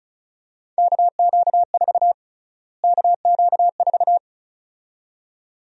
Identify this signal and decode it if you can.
Morse code, heard as a keyed tone, about 36 wpm, 700 Hz. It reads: KQ4 KQ4
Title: USA Call Sign Prefixes - Repeated 2x - 35wpm